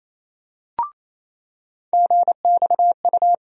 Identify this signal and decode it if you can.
Morse code, heard as a keyed tone, about 28 wpm, 700 Hz. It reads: GXU
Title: USA Call Sign Postfixes - 3 Letters 28wpm